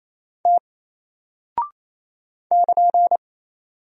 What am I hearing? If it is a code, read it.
Morse code, heard as a keyed tone, about 28 wpm, 700 Hz. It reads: T <KN>
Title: Single Character (Letter:Number:Punctuation:Prosign) Alt 28wpm